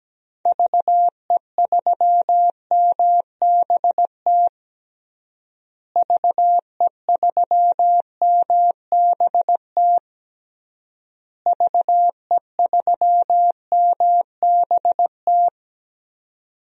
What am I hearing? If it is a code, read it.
Morse code, heard as a keyed tone, about 17 wpm, 700 Hz. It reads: VE3MBT VE3MBT VE3MBT
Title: DXCC-100 Call Signs - 6 Characters - Repeated 3x - 17wpm